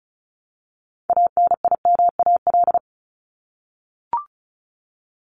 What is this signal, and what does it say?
Morse code, heard as a keyed tone, about 34 wpm, 700 Hz. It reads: ANIMAL
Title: ES:S Suffix - Top 100 Words with Base Words 35wpm